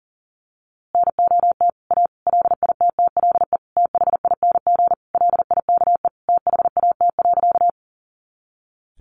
Morse code, heard as a keyed tone, about 40 wpm, 700 Hz. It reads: NOT A LITTLE THING LIKE THAT.
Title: A Sound Of Thunder 40wpm